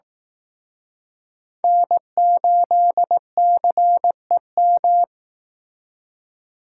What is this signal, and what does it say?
Morse code, heard as a keyed tone, about 18 wpm, 700 Hz. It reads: N8CEM
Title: Speed Racing - USA 1x3 Call Signs 15wpm